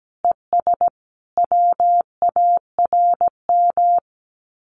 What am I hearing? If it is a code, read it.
Morse code, heard as a keyed tone, about 17 wpm, 700 Hz. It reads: ES WARM
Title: Common QSO Phrases Encore 17wpm